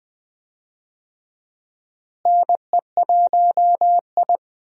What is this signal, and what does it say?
Morse code, heard as a keyed tone, about 20 wpm, 700 Hz. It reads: NE1I
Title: USA Call Signs - 4 Characters - Repeated 2x - 20wpm